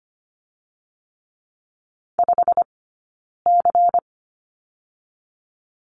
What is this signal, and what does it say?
Morse code, heard as a keyed tone, about 25 wpm, 700 Hz. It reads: 5 C
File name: course / 25:10wpm / 092 Lesson 16d - Letter c - Two Characters 25wpm